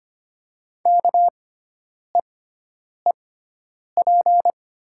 Morse code, heard as a keyed tone, about 25 wpm, spaced as for 9 wpm, 690 Hz. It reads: KEEP
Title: 6x Character Spacing - Top 500 Words - 25wpm